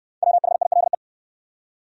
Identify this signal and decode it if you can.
Morse code, heard as a keyed tone, about 69 wpm, 700 Hz. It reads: WHILE